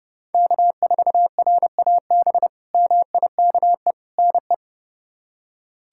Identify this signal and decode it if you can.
Morse code, heard as a keyed tone, about 30 wpm, 700 Hz. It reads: K4RAB MIKE NE
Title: K1USN SST QSO Practice - 30wpm